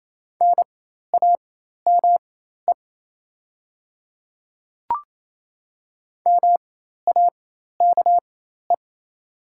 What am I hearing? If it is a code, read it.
Morse code, heard as a keyed tone, about 27 wpm, spaced as for 13 wpm, 690 Hz. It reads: NAME MAKE
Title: 4x Character Spacing - Top 300 Words - 28wpm